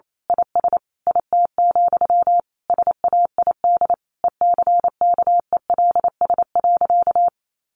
Morse code, heard as a keyed tone, about 28 wpm, 700 Hz. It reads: IS IT, SAID ECKELS.
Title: A Sound Of Thunder 28wpm